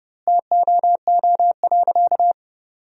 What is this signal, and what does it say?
Morse code, heard as a keyed tone, about 32 wpm, 700 Hz. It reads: TOO.
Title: 4x Word-Spacing - Sentences from Top 400 Words - 30wpm